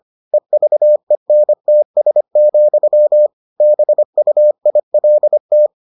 Morse code, heard as a keyed tone, about 25 wpm, 600 Hz. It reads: EVENTS, BUILT